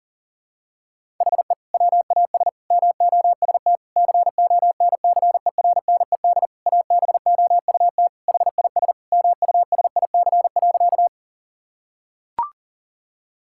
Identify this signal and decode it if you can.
Morse code, heard as a keyed tone, about 40 wpm, 700 Hz. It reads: HE WAS MOST CONCERNED ABOUT HIS MUSIC.
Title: Sentences from Top 1000 Words - Review Part 6 - 40wpm